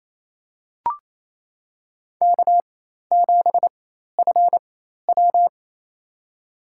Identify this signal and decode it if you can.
Morse code, heard as a keyed tone, about 28 wpm, spaced as for 13 wpm, 700 Hz. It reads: K7FW
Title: USA Call Signs - 4 Characters - 4x Character Spacing - 28wpm